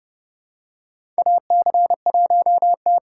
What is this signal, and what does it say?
Morse code, heard as a keyed tone, about 31 wpm, 700 Hz. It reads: AC1T
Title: USA Call Signs - 2x Character Spacing - 30wpm